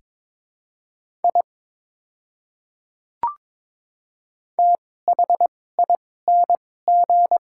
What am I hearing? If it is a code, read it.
Morse code, heard as a keyed tone, about 22 wpm, spaced as for 16 wpm, 700 Hz. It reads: I THING
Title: 2x Character Spacing - Top 100 Words - 22wpm